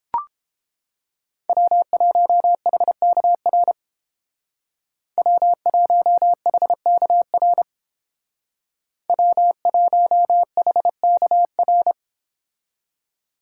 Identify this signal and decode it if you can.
Morse code, heard as a keyed tone, about 29 wpm, 700 Hz. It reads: W1HKR W1HKR W1HKR